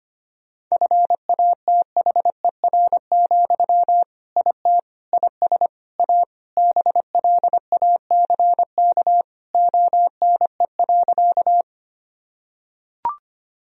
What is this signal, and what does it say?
Morse code, heard as a keyed tone, about 25 wpm, 700 Hz. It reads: FATHER, IT IS A BLACK ONE.
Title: Sentences from Top 1000 Words - Review Part 2 - 25wpm